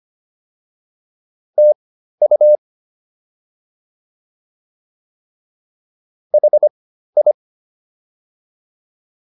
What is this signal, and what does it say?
Morse code, heard as a keyed tone, about 25 wpm, spaced as for 13 wpm, 600 Hz. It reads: TU HI